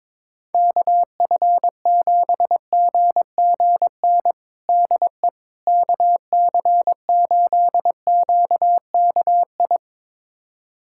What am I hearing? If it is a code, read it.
Morse code, heard as a keyed tone, about 22 wpm, 700 Hz. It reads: KF7GGN DE KC8QKI
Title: USA Callsign DE Callsign 22wpm